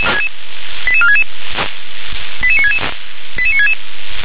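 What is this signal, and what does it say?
Five-tone selective call (ZVEI), heard as quick runs of five tones, tones of about 80 ms. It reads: 88466 88966 88966
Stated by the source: ZVEI five-tone selective call received on SDR.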